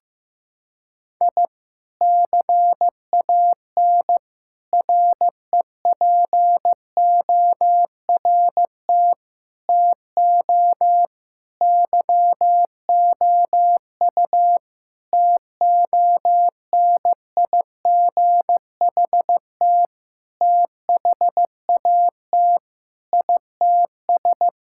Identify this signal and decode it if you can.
Morse code, heard as a keyed tone, about 15 wpm, 700 Hz. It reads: I CAN REPORT TO YOU TONIGHT THAT ITS
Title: Sentences from Top 1000 Words - Review Part 6 - 15wpm